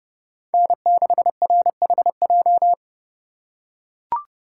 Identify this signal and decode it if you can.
Morse code, heard as a keyed tone, about 31 wpm, 700 Hz. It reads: N6RHJ